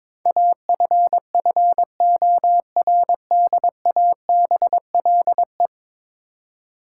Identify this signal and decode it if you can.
Morse code, heard as a keyed tone, about 21 wpm, 700 Hz. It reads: AFFORDABLE
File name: practice / 22wpm / 10 Letter Words - First 3 Letters Given - 22wpm